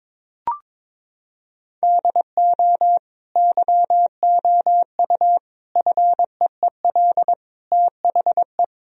Morse code, heard as a keyed tone, about 22 wpm, 700 Hz. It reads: DO YOU FEEL THE
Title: Sentences from Top 1000 Words - Review Part 3 - 22wpm